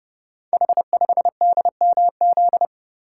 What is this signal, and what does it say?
Morse code, heard as a keyed tone, about 29 wpm, 700 Hz. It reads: H5DMZ
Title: Code Groups - 5 Letter:Numbers - 30wpm